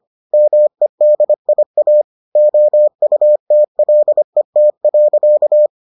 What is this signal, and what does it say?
Morse code, heard as a keyed tone, about 25 wpm, 600 Hz. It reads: MEDIA OUTLET.